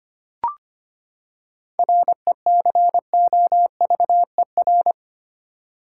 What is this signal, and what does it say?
Morse code, heard as a keyed tone, about 25 wpm, 700 Hz. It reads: RECOVER
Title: Seven Letter Words - 25wpm